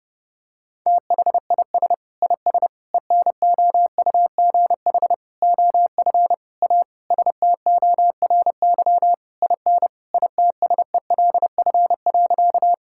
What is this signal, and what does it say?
Morse code, heard as a keyed tone, about 30 wpm, 700 Hz. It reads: THIS IS ENOUGH OF A STORY IN ITSELF.
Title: Sentences from Top 1000 Words - Review Part 5 - 30wpm